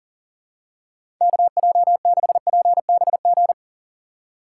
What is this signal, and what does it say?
Morse code, heard as a keyed tone, about 38 wpm, 700 Hz. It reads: KJ6PBG
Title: USA Call Signs - 2x Character Spacing - 40wpm